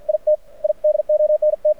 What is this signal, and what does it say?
Morse code, heard as a keyed tone, about 50 wpm, 600 Hz. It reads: IT I DONT
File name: archived / CRSnet